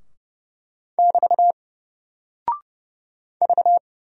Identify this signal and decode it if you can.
Morse code, heard as a keyed tone, about 30 wpm, 700 Hz. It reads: <BT> V
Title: Single Character (Letter:Number:Punctuation:Prosign) Alt 30wpm